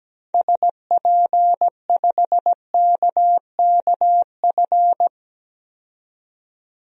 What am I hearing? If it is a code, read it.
Morse code, heard as a keyed tone, about 17 wpm, 700 Hz. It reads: SP5KKF
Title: DXCC-100 Call Signs - 6 Characters - 17wpm